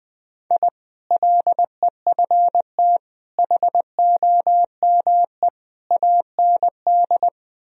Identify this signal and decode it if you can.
Morse code, heard as a keyed tone, about 20 wpm, 700 Hz. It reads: I LEFT HOME AND
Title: Sentences from Top 1000 Words - Review Part 3 - 20wpm